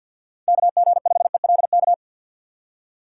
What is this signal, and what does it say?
Morse code, heard as a keyed tone, about 47 wpm, 700 Hz. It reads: KG5ELK
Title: USA Call Signs - 3x Character Spacing - 50wpm